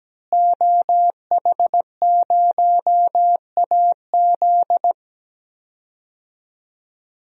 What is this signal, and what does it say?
Morse code, heard as a keyed tone, about 17 wpm, 700 Hz. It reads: OH0AZ
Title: DXCC-100 Call Signs - 5 Characters - Repeated 3x - 17wpm